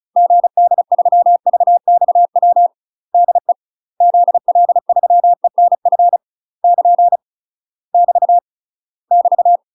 Morse code, heard as a keyed tone, about 36 wpm, 700 Hz. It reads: GD3VXW DE ZL3ENF <KN> <BT> <BT>
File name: UK MASTER QSOs TRAINING / 35WPM / UK MASTER QSOs 0019